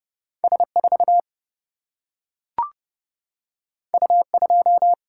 Morse code, heard as a keyed tone, about 30 wpm, 700 Hz. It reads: S4 U2
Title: Single-Letter Single-Number 30wpm